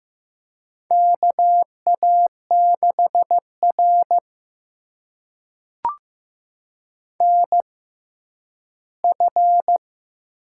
Morse code, heard as a keyed tone, about 15 wpm, spaced as for 10 wpm, 700 Hz. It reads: KA6R NF
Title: USA Call Signs - 6x Character Spacing - 15wpm